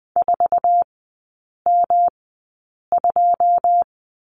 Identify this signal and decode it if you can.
Morse code, heard as a keyed tone, about 20 wpm, spaced as for 8 wpm, 700 Hz. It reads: 4M2
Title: 107 Lesson 18e - Letter m - Three Characters 20wpm